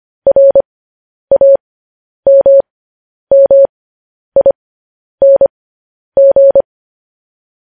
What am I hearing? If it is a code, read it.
Morse code, heard as a keyed tone, about 25 wpm, spaced as for 10 wpm, 550 Hz. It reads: RAMMING
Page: news headlines in morse code at 10 wpm